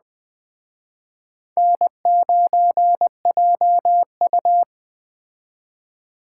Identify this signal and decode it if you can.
Morse code, heard as a keyed tone, about 20 wpm, 700 Hz. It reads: N9JU